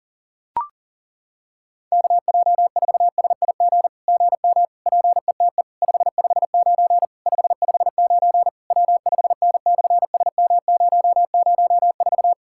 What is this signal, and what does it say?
Morse code, heard as a keyed tone, about 40 wpm, 700 Hz. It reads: KJ4SIG GM PETE 559 559 W5N/SM004
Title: SOTA QSO Practice 40wpm